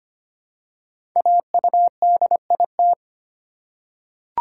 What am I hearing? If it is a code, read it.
Morse code, heard as a keyed tone, about 25 wpm, 700 Hz. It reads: AUDIT
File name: 5 Letter Words - First 3 Letters Given - 25wpm